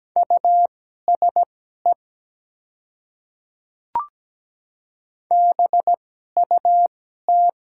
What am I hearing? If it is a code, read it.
Morse code, heard as a keyed tone, about 17 wpm, spaced as for 12 wpm, 700 Hz. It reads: USE BUT
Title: 2x Character Spacing - Top 100 Words - 17wpm